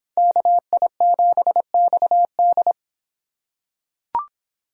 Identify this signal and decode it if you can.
Morse code, heard as a keyed tone, about 25 wpm, 700 Hz. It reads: KI7XD